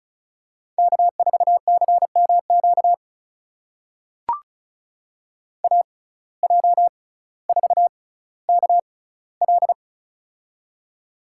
Morse code, heard as a keyed tone, about 36 wpm, spaced as for 18 wpm, 700 Hz. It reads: K4CMQ AJ4KL
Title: USA Call Signs - 5 Characters - 6x Character Spacing - 35wpm